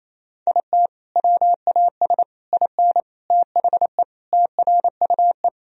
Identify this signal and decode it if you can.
Morse code, heard as a keyed tone, about 28 wpm, 700 Hz. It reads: IT WAS IN THE TRUE
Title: Sentences from Top 1000 Words - Review Part 4 - 28wpm